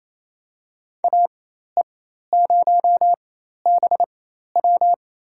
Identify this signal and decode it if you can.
Morse code, heard as a keyed tone, about 27 wpm, spaced as for 13 wpm, 700 Hz. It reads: AE0BW
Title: USA Call Signs - 5 Characters - 4x Character Spacing - 28wpm